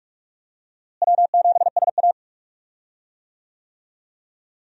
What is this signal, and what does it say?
Morse code, heard as a keyed tone, about 46 wpm, 700 Hz. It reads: W7SA
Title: USA Call Signs - 4 Characters - Repeated 2x - 45wpm